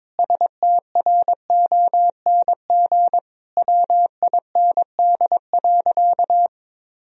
Morse code, heard as a keyed tone, about 22 wpm, 700 Hz. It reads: STRONG WIND.
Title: Sentences from Top 1000 Words - Review Part 4 - 22wpm